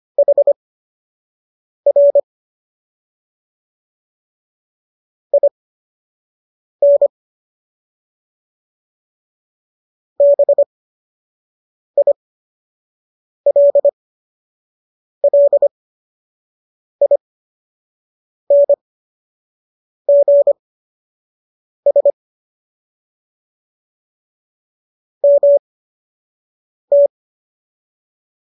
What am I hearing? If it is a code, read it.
Morse code, heard as a keyed tone, about 25 wpm, spaced as for 6 wpm, 570 Hz. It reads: HR IN BILLINGS MT